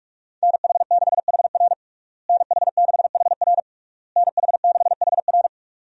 Morse code, heard as a keyed tone, about 46 wpm, 700 Hz. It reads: NH6HR NH6HR NH6HR
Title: USA Call Signs - 5 Characters - Repeated 3x - 45wpm